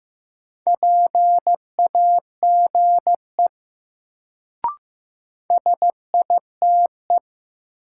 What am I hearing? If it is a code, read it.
Morse code, heard as a keyed tone, about 15 wpm, 700 Hz. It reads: PAGE SITE
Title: Sentences from Top 1000 Words - Review Part 1 - New Word Drill - 15wpm